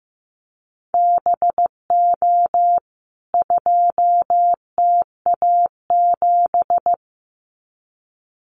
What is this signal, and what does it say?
Morse code, heard as a keyed tone, about 15 wpm, 700 Hz. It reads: BO 2TA7
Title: CWT Exchange 15wpm